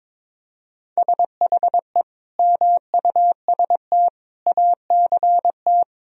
Morse code, heard as a keyed tone, about 22 wpm, 700 Hz. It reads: SHE MUST ACT
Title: Sentences from Top 1000 Words - Review Part 1 - 22wpm